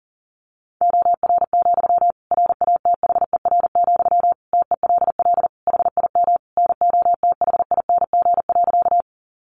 Morse code, heard as a keyed tone, about 40 wpm, 700 Hz. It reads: OR, RATHER, TELL HIM NOTHING.
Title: Sentences from Top 700 Words 40wpm